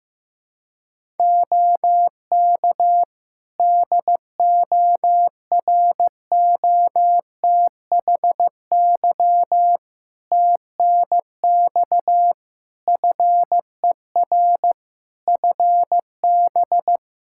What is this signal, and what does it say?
Morse code, heard as a keyed tone, about 16 wpm, 700 Hz. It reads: OK DOROTHY TNX FER FB
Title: Practice QSO - Exchange 4 - Ending - 15wpm